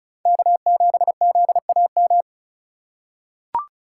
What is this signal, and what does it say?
Morse code, heard as a keyed tone, about 36 wpm, 700 Hz. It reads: K7ZAM
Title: USA Call Signs - 5 Characters - 4x Character Spacing - 35wpm